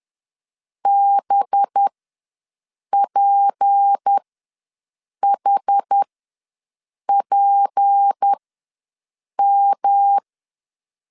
Morse code, heard as a keyed tone, about 10 wpm, spaced as for 6 wpm, 800 Hz. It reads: BPHPM
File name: LEZIONI / 6 PKX0 / 6 P ws 10wpm 3751